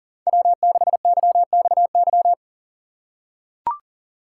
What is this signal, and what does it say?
Morse code, heard as a keyed tone, about 40 wpm, 700 Hz. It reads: W6YXY
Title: USA Call Signs - 5 Characters - 2x Character Spacing - 40wpm